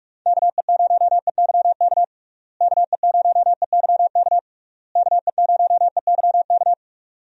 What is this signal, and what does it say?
Morse code, heard as a keyed tone, about 46 wpm, 700 Hz. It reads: KE0EYK KE0EYK KE0EYK
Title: USA Call Signs - 6 Characters - Repeated 3x - 45wpm